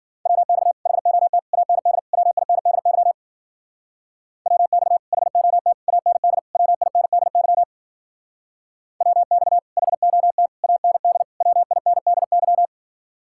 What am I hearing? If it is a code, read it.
Morse code, heard as a keyed tone, about 50 wpm, 700 Hz. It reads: WX HOT AND WINDY WX HOT AND WINDY WX HOT AND WINDY